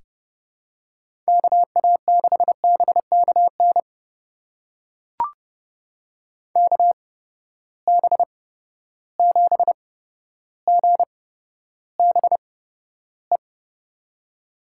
Morse code, heard as a keyed tone, about 30 wpm, spaced as for 13 wpm, 700 Hz. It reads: KA6BKN KB7GBE
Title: USA Call Signs - 6 Characters - 8x Character Spacing - 30wpm